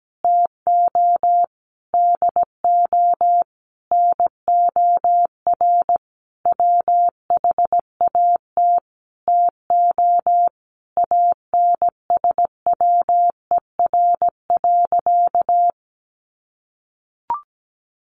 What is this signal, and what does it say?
Morse code, heard as a keyed tone, about 17 wpm, 700 Hz. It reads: TO DO NOR WHAT TO ANSWER.
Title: Sentences from Top 1500 Words 17wpm